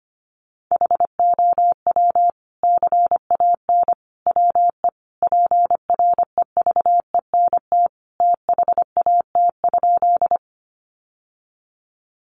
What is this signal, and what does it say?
Morse code, heard as a keyed tone, about 25 wpm, 700 Hz. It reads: HOW CAN WE PREVENT THAT?
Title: Sentences from Top 1100 Words 25wpm